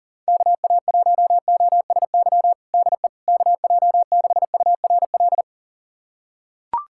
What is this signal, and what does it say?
Morse code, heard as a keyed tone, about 40 wpm, 700 Hz. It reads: KA1OSY DE KJ6URL
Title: USA Callsign DE Callsign - 6 Characters - 40wpm